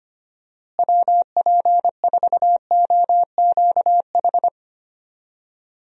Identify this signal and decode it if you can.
Morse code, heard as a keyed tone, about 25 wpm, 700 Hz. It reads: WP4OQH
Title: USA Call Signs - 6 Characters - Repeated 2x - 25wpm